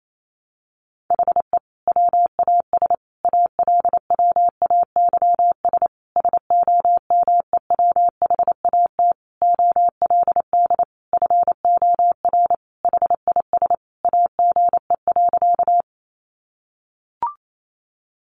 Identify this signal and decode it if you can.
Morse code, heard as a keyed tone, about 28 wpm, 700 Hz. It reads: HE WAS ALWAYS SOMEWHAT OLD FOR HIS AGE.
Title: Sentences from Top 1300 Words 28wpm